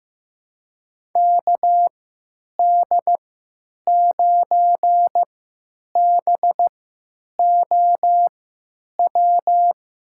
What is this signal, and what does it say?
Morse code, heard as a keyed tone, about 16 wpm, spaced as for 9 wpm, 700 Hz. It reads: KD9BOW
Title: USA Call Signs - 6 Characters - 3x Character Spacing - 15wpm